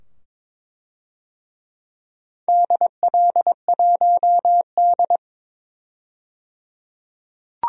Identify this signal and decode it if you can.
Morse code, heard as a keyed tone, about 23 wpm, 700 Hz. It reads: DL1D
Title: DXCC-100 Call Signs - 4 Characters - 22wpm